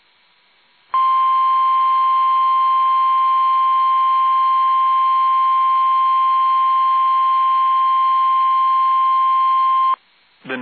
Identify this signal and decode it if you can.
Emergency Alert System (SAME) tones (the 1050 Hz attention tone), with speech. It is heard from 0.9 to 10.0 s.
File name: eas / KUSO / KUSO